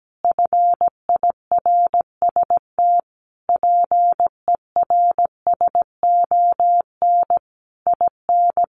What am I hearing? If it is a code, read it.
Morse code, heard as a keyed tone, about 17 wpm, 700 Hz. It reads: FIRST PERSON IN
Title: Sentences from Top 400 Words 17wpm